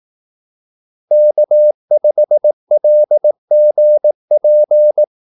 Morse code, heard as a keyed tone, about 18 wpm, 600 Hz. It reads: K5LGP
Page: Kaldesignaler 18 wpm